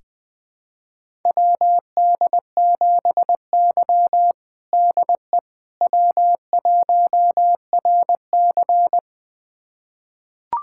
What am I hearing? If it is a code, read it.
Morse code, heard as a keyed tone, about 21 wpm, 700 Hz. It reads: WD7Y DE W1RC
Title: USA Callsign DE Callsign - 4 Characters - 20wpm